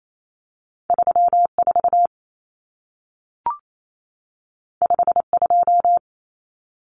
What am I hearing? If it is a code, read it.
Morse code, heard as a keyed tone, about 28 wpm, 700 Hz. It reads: 34 52
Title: Numbers - 2-Digit 28wpm